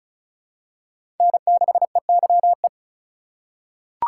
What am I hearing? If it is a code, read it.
Morse code, heard as a keyed tone, about 36 wpm, 700 Hz. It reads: N6EYE
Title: USA Call Signs - 5 Characters - 3x Character Spacing - 35wpm